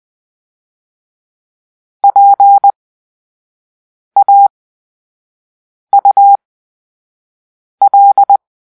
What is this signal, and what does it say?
Morse code, heard as a keyed tone, about 20 wpm, spaced as for 5 wpm, 800 Hz. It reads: PAUL
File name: Grp-3 Drill-4